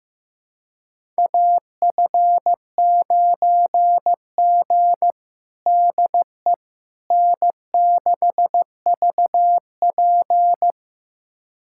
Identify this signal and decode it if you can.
Morse code, heard as a keyed tone, about 15 wpm, 700 Hz. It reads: AF9G DE N6VP
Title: USA Callsign DE Callsign - 4 Characters - 15wpm